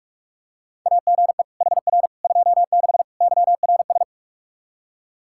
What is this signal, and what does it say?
Morse code, heard as a keyed tone, about 46 wpm, 700 Hz. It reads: AGE HR 26 YRS
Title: Practice QSO - Exchange 3 - More Info - 45wpm